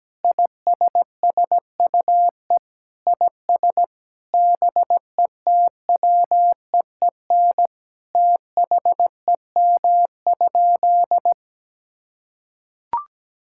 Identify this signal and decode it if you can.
Morse code, heard as a keyed tone, about 17 wpm, 700 Hz. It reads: ISSUE IS BETWEEN THEM?
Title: Sentences from Top 1000 Words - Review Part 6 - 17wpm